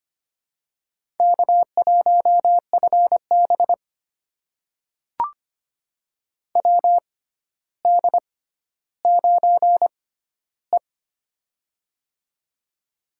Morse code, heard as a keyed tone, about 25 wpm, spaced as for 13 wpm, 700 Hz. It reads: K1FB WD9E
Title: USA Call Signs - 4 Characters - 6x Character Spacing - 25wpm